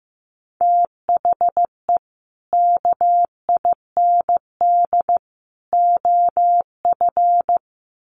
Morse code, heard as a keyed tone, about 15 wpm, 700 Hz. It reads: THE KIND OF
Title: Sets of 3 Words 15wpm